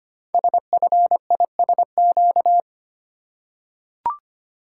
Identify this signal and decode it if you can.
Morse code, heard as a keyed tone, about 25 wpm, 700 Hz. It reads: SFISQ
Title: Code Groups - 5 Letter:Numbers - 25wpm